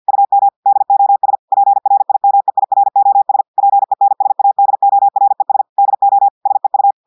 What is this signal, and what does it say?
Morse code, heard as a keyed tone, about 50 wpm, 800 Hz. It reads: UM DOS PRIMEIROS PENSADORES DO SEU
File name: audios / portugais / portugais texte en clair 1 vitesse 50 mots minutes